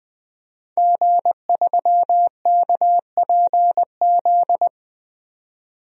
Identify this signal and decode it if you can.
Morse code, heard as a keyed tone, about 20 wpm, 700 Hz. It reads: G3KPZ
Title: DXCC-100 Call Signs - 5 Characters - 20wpm